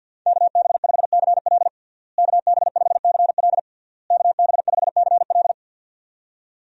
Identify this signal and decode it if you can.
Morse code, heard as a keyed tone, about 49 wpm, 700 Hz. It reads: KB5CL KB5CL KB5CL
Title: USA Call Signs - 5 Characters - Repeated 3x - 50wpm